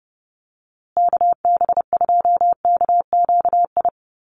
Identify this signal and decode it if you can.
Morse code, heard as a keyed tone, about 30 wpm, 700 Hz. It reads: KB2KQI
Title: USA Call Signs Repeated 3x 30wpm